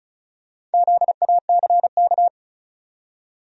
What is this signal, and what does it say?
Morse code, heard as a keyed tone, about 36 wpm, 700 Hz. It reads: ZACK
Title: USA Common Names - Max 5 Letters - 35wpm